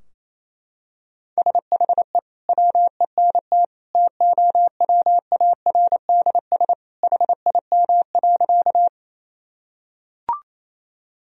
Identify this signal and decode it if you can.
Morse code, heard as a keyed tone, about 28 wpm, 700 Hz. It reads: SHE WENT TOWARDS HIM.
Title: Sentences from Top 1000 Words - Review Part 2 - 28wpm